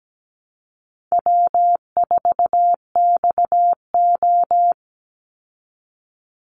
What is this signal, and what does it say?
Morse code, heard as a keyed tone, about 17 wpm, 700 Hz. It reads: W4XO
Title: Easy Call Signs Repeated 3x - 17wpm